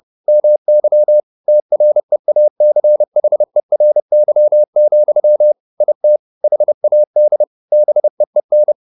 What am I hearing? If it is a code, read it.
Morse code, heard as a keyed tone, about 30 wpm, 600 Hz. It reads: MY TREACHERY, IT HAD BEEN
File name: the old man and the sea / the old man and the sea 1 30wpm